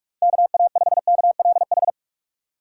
Morse code, heard as a keyed tone, about 45 wpm, 700 Hz. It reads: KA5KLH
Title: USA Call Signs - 6 Characters - 4x Character Spacing - 45wpm